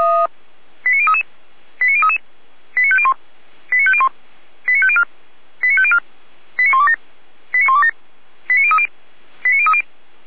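Five-tone selective call (ZVEI), heard as quick runs of five tones, tones of about 70 ms. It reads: 89033 89033 89501 89501 89504 89504 89127 89127 89030 89030
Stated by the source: ZVEI five-tone selective call received by SDR.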